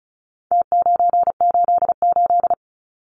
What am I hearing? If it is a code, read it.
Morse code, heard as a keyed tone, about 34 wpm, 700 Hz. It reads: T988
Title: Numbers - Contest Style - 4-Digit 35wpm